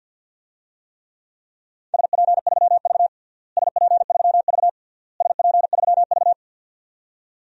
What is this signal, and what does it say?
Morse code, heard as a keyed tone, about 52 wpm, 700 Hz. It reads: SP3V SP3V SP3V
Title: DXCC-100 Call Signs - 4 Characters - Repeated 3x - 50wpm